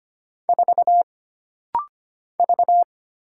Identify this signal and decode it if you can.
Morse code, heard as a keyed tone, about 25 wpm, 700 Hz. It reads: 4 V
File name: Single Character (Letter:Number:Punctuation:Prosign) - Rapid-Fire - 25wpm